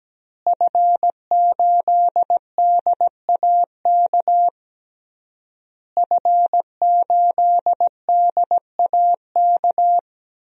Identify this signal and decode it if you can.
Morse code, heard as a keyed tone, about 17 wpm, 700 Hz. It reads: F8DAK F8DAK
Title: DXCC-100 Call Signs - 5 Characters - Repeated 3x - 17wpm